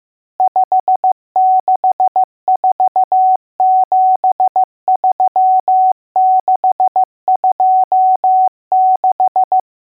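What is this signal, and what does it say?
Morse code, heard as a keyed tone, about 15 wpm, 750 Hz. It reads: 56473626
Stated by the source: Morse code lesson: test on all digits.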